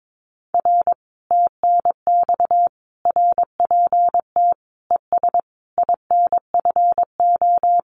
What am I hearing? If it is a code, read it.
Morse code, heard as a keyed tone, about 22 wpm, 700 Hz. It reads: R TNX RPT ES INFO
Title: Common QSO Phrases Encore 22wpm